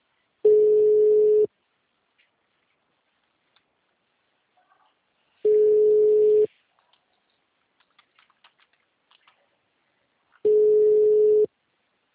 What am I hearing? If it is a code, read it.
Ringback tone (Europe, 425 Hz).